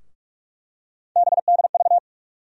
Morse code, heard as a keyed tone, about 42 wpm, 700 Hz. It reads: BDV
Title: USA Call Sign Postfixes - 3 Letters 45wpm